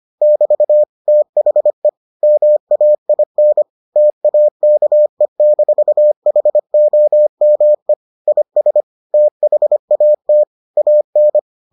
Morse code, heard as a keyed tone, about 25 wpm, 600 Hz. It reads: <BT> THE MAIN TAKE-HOME IS THAT AN